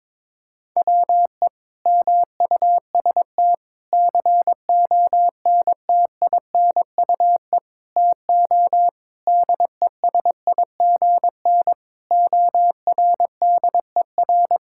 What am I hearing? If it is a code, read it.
Morse code, heard as a keyed tone, about 22 wpm, 700 Hz. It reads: WE MUST CONTINUE TO DESIGN ORDER
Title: Sentences from Top 1000 Words - Review Part 6 - 22wpm